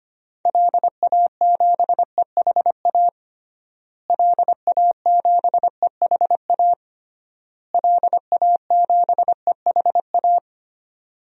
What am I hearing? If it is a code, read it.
Morse code, heard as a keyed tone, about 25 wpm, 700 Hz. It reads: LA7EHA LA7EHA LA7EHA
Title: DXCC-100 Call Signs - 6 Characters - Repeated 3x - 25wpm